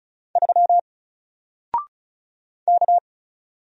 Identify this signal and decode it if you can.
Morse code, heard as a keyed tone, about 34 wpm, 700 Hz. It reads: 3 K
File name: Single-Letter-Number-35wpm